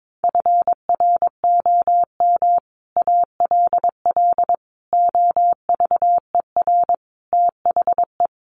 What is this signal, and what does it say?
Morse code, heard as a keyed tone, about 22 wpm, 700 Hz. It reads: FROM ALL OVER THE
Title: Sets of 5 Words Encore 22wpm